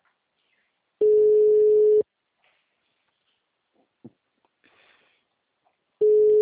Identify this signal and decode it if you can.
Ringback tone (Europe, 425 Hz).
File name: out- -2005- - - .83274